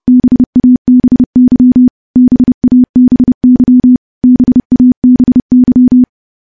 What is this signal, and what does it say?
Morse code, heard as a keyed tone, about 31 wpm, 250 Hz. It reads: BABY BABY BABY
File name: morse / morse-training-30WPM